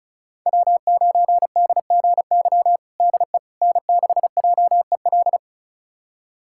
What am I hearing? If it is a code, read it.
Morse code, heard as a keyed tone, about 36 wpm, 700 Hz. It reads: W9DGY DE N6JEL
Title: USA Callsign DE Callsign - 5 Characters - 35wpm